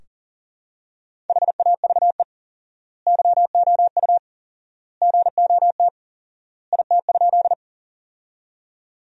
Morse code, heard as a keyed tone, about 40 wpm, 700 Hz. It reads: HAVE YOU GOT IT?
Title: 4x Word-Spacing - Sentences from Top 300 Words - 40wpm